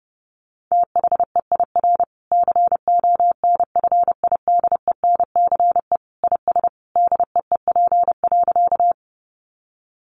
Morse code, heard as a keyed tone, about 28 wpm, 700 Hz. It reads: THEIR CONFIDENCE IS DEEP.
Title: Sentences from Top 1500 Words 30wpm